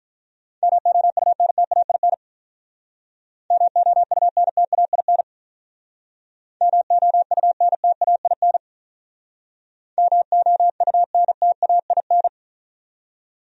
Morse code, heard as a keyed tone, about 44 wpm, 700 Hz. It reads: MOUNTAIN MOUNTAIN MOUNTAIN MOUNTAIN